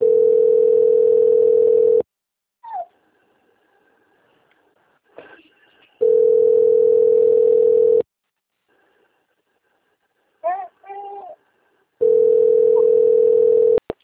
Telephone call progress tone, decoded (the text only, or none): ringback tone